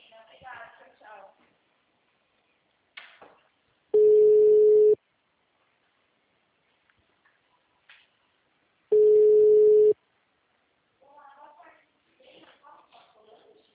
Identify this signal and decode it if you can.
Ringback tone (Europe, 425 Hz), with speech.
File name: out- -2014- - - .29730